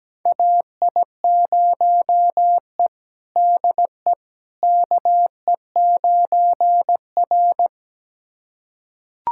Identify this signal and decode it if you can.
Morse code, heard as a keyed tone, about 17 wpm, 700 Hz. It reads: AI0E DE KE9R
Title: USA Callsign DE Callsign - 4 Characters - 17wpm